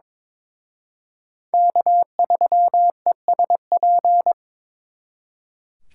Morse code, heard as a keyed tone, about 26 wpm, 700 Hz. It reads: K3ESP
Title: Speed Racing - USA 1x3 Call Signs 22wpm